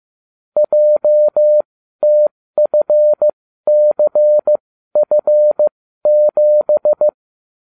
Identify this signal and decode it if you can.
Morse code, heard as a keyed tone, about 15 wpm, spaced as for 12 wpm, 600 Hz. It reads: JTFCF7